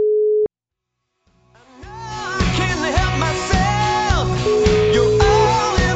Ringback tone (Europe, 425 Hz), with noise.